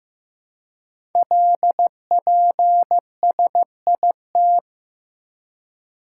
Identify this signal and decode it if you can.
Morse code, heard as a keyed tone, about 15 wpm, 700 Hz. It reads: LPSIT
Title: Code Groups - 5 Letters - 15wpm